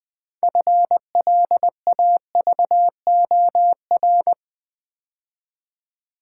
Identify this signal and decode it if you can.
Morse code, heard as a keyed tone, about 20 wpm, 700 Hz. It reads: FLAVOR
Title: 6 Letter Words - First 3 Letters Given - 20wpm